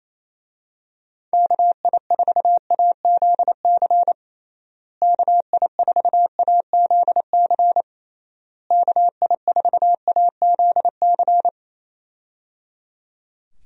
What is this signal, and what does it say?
Morse code, heard as a keyed tone, about 27 wpm, 700 Hz. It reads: KI4AZC KI4AZC KI4AZC
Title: USA Call Signs - 6 Characters - Repeated 3x - 28wpm